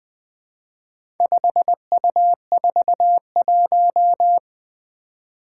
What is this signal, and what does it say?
Morse code, heard as a keyed tone, about 20 wpm, 700 Hz. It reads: 5U41
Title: Four Challenging Letter-Number Combinations 20wpm